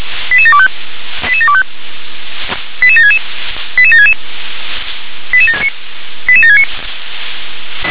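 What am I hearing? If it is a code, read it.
Five-tone selective call (ZVEI), heard as quick runs of five tones, tones of about 70 ms. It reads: 88725 88725 88766 88766 88769
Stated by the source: ZVEI five-tone call received by SDR.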